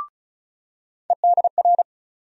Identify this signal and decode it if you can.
Morse code, heard as a keyed tone, about 36 wpm, 700 Hz. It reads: EDR
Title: USA Call Sign Postfixes - 3 Letters 35wpm